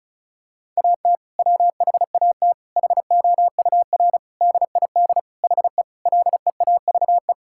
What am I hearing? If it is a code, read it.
Morse code, heard as a keyed tone, about 34 wpm, 700 Hz. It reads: AT WHAT HOUR DID HE LEAVE
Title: Sentences from Top 1600 Words 35wpm